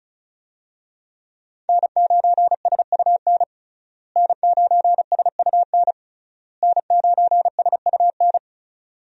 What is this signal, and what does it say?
Morse code, heard as a keyed tone, about 36 wpm, 700 Hz. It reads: N9SUN N9SUN N9SUN
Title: USA Call Signs - 5 Characters - Repeated 3x - 35wpm